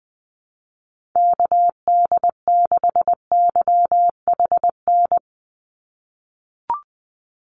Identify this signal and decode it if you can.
Morse code, heard as a keyed tone, about 20 wpm, 700 Hz. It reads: KD6YHN
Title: USA Call Signs 20wpm